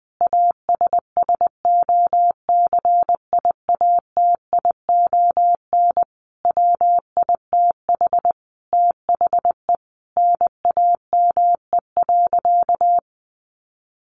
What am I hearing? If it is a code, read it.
Morse code, heard as a keyed tone, about 20 wpm, 700 Hz. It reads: ASSOCIATION WITH THE NAME.
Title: Sentences from Top 1100 Words 20wpm